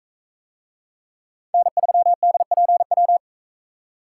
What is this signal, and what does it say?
Morse code, heard as a keyed tone, about 40 wpm, 700 Hz. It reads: N3DPW